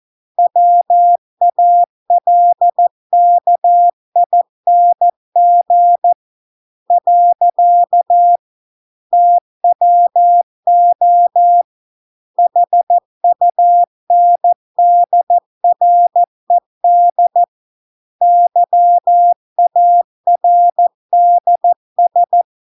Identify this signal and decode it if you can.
Morse code, heard as a keyed tone, about 14 wpm, 700 Hz. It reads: WALKING . TWO HUNDRED YARDS